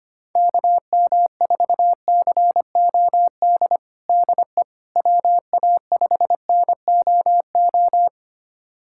Morse code, heard as a keyed tone, about 25 wpm, 700 Hz. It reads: KM4COD DE WA5NOO
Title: USA Callsign DE Callsign - 6 Characters - 25wpm